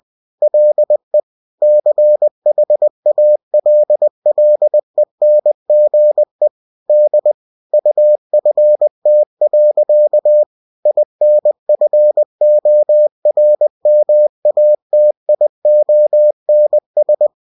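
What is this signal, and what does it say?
Morse code, heard as a keyed tone, about 20 wpm, 600 Hz. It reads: LE CHALLENGE D UFT. INFORMATIONS
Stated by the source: UFT bulletin sent in CW.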